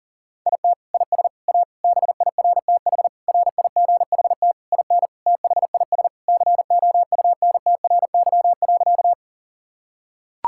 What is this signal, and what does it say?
Morse code, heard as a keyed tone, about 40 wpm, 700 Hz. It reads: IT IS A BIRTH RIGHT IN THIS COUNTRY.
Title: Sentences from Top 1000 Words - Review Part 5 - 40wpm